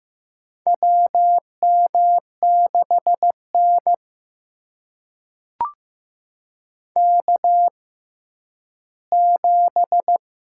Morse code, heard as a keyed tone, about 15 wpm, spaced as for 10 wpm, 700 Hz. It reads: WM6N K7
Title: USA Call Signs - 4 Characters - 6x Character Spacing - 15wpm